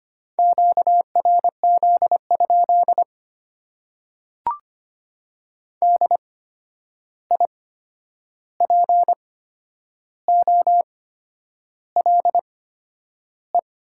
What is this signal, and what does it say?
Morse code, heard as a keyed tone, about 25 wpm, spaced as for 9 wpm, 700 Hz. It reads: QRZ? DIPOLE
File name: Top 90 QSO Elements - 8x Character Spacing - 25wpm